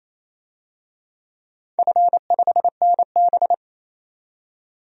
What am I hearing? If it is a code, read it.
Morse code, heard as a keyed tone, about 27 wpm, 690 Hz. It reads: F5NB
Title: DXCC-100 Call Signs - 4 Characters - 28wpm